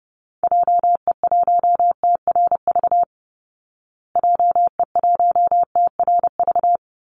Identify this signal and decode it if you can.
Morse code, heard as a keyed tone, about 30 wpm, 700 Hz. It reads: JE1TRV JE1TRV
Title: Call Signs Repeated 3x - 30wpm